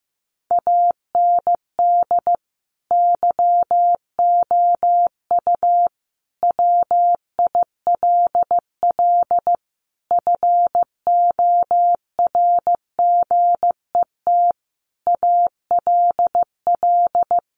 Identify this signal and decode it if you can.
Morse code, heard as a keyed tone, about 14 wpm, 700 Hz. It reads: AND YOU WILL FORGET ALL
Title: Sentences from Top 1200 Words 15wpm